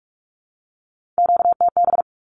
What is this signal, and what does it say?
Morse code, heard as a keyed tone, about 44 wpm, 700 Hz. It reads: QTB
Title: Q-Codes Repeated 3x 45wpm